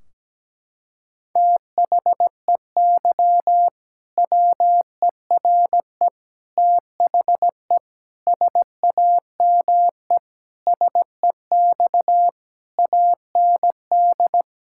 Morse code, heard as a keyed tone, about 17 wpm, 700 Hz. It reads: THEY WERE THE SAME SEX AND
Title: Sentences from Top 1600 Words 17wpm